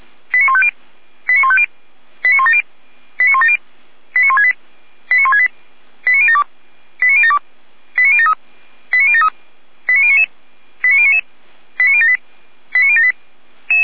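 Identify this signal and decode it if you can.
Five-tone selective call (ZVEI), heard as quick runs of five tones, tones of about 80 ms. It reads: 89150 89150 89170 89170 89168 89168 89072 89072 89073 89073 89099 89099 89078 89078
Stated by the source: ZVEI five-tone selective call received on SDR.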